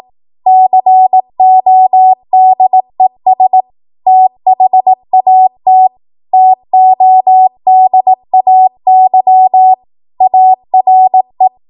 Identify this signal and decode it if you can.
Morse code, heard as a keyed tone, about 18 wpm, 750 Hz. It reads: CODES THAT TODAY ARE
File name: Morse / 18WPM